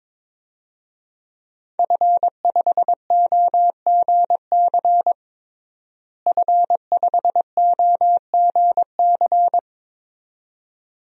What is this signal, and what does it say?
Morse code, heard as a keyed tone, about 22 wpm, 700 Hz. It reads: F5OGC F5OGC
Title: DXCC-100 Call Signs - 5 Characters - Repeated 2x - 22wpm